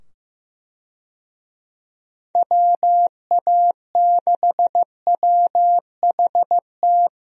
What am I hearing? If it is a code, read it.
Morse code, heard as a keyed tone, about 16 wpm, 700 Hz. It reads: WA6WHT
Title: USA Call Signs - 6 Characters - 15wpm